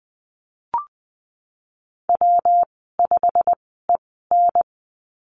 Morse code, heard as a keyed tone, about 20 wpm, spaced as for 15 wpm, 700 Hz. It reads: W5EN
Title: 193 Lesson 31g - Letter v - Call Signs 20wpm